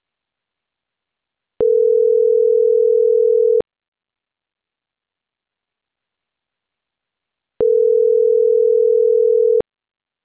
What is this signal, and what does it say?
Ringback tone (North America), 440 + 480 Hz.